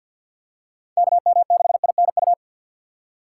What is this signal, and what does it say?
Morse code, heard as a keyed tone, about 50 wpm, 700 Hz. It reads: KM6INU
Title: USA Call Signs - 6 Characters - 3x Character Spacing - 50wpm